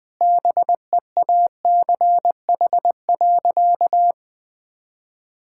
Morse code, heard as a keyed tone, about 20 wpm, 700 Hz. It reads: BEACH.
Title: Sentences with 6-Word Phrases - 20wpm